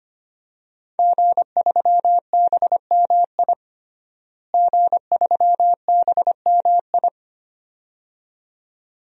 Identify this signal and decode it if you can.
Morse code, heard as a keyed tone, about 25 wpm, 700 Hz. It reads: G3BMI G3BMI
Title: DXCC-100 Call Signs - 5 Characters - Repeated 2x - 25wpm